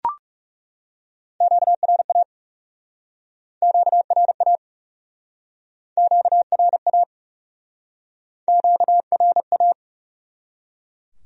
Morse code, heard as a keyed tone, about 36 wpm, 700 Hz. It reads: QRA QRA QRA QRA
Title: Speed-Racing Q-Codes - 30wpm